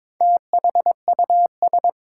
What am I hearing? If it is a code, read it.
Morse code, heard as a keyed tone, about 22 wpm, 700 Hz. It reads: THUS
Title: Sentences from Top 800 Words - New Word Drill - 22wpm